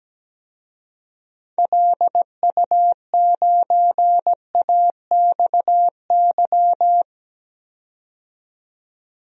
Morse code, heard as a keyed tone, about 17 wpm, 700 Hz. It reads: LU9AXY
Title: DXCC-100 Call Signs - 6 Characters - 17wpm